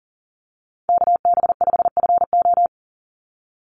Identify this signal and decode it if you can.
Morse code, heard as a keyed tone, about 40 wpm, 700 Hz. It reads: KB5FO
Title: USA Call Signs 40wpm